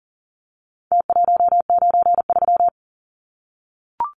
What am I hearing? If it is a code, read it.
Morse code, heard as a keyed tone, about 40 wpm, 700 Hz. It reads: T193
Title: Numbers - Contest Style - 4-Digit 40wpm